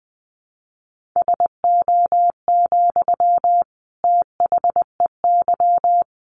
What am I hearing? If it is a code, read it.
Morse code, heard as a keyed tone, about 20 wpm, 700 Hz. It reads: SO, THEY
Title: Sentences from Top 1400 Words 20wpm